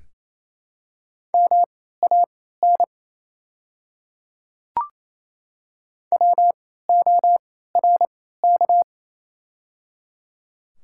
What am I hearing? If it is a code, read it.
Morse code, heard as a keyed tone, about 28 wpm, spaced as for 16 wpm, 700 Hz. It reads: MAN WORK
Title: 3x Character Spacing - Top 200 Words - 28wpm